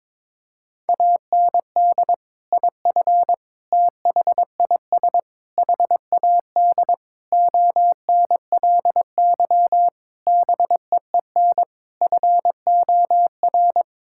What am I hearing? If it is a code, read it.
Morse code, heard as a keyed tone, about 22 wpm, 700 Hz. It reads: AND IF THIS HAD ONLY BEEN FOR
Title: Sentences from Top 1600 Words 22wpm